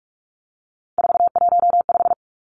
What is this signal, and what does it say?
Morse code, heard as a keyed tone, about 44 wpm, 700 Hz. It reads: 415
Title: Numbers - 3-Digit 45wpm